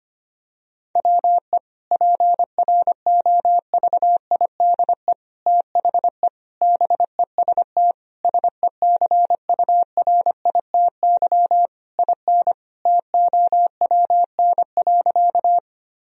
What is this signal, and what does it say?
Morse code, heard as a keyed tone, about 25 wpm, 700 Hz. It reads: WE PROVIDE THE BEST SECURITY IN TOWN.
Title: Sentences from Top 1000 Words - Review Part 4 - 25wpm